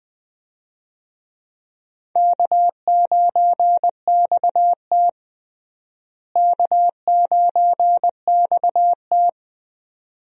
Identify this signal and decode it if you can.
Morse code, heard as a keyed tone, about 21 wpm, 700 Hz. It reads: K9XT K9XT
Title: USA Call Signs - 4 Characters - Repeated 2x - 20wpm